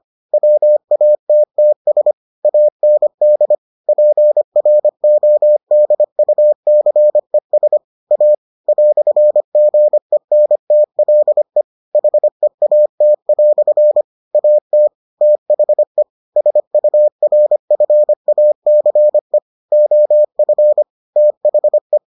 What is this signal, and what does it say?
Morse code, heard as a keyed tone, about 25 wpm, 600 Hz. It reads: WATTS AND PRODUCES A "GENTLE HEAT" AT THE SURFACE OF THE